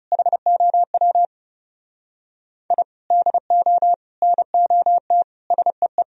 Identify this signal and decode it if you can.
Morse code, heard as a keyed tone, about 28 wpm, 700 Hz. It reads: HOW I DO NOT SEE
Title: Speed-Racing - Sets of 5 Words 30wpm